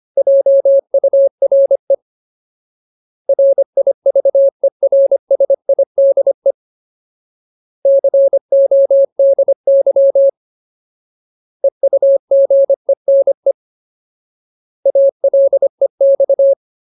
Morse code, heard as a keyed tone, about 25 wpm, 550 Hz. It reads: JURE RIVERSIDE CODY EUGENE ALEX
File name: Ultimate Word List / 25WPM Ultimate